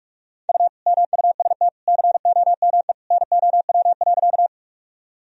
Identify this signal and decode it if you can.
Morse code, heard as a keyed tone, about 46 wpm, 700 Hz. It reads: U MUST COME NOW.
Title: Sentences from Top 200 Words and Abbreviations 45wpm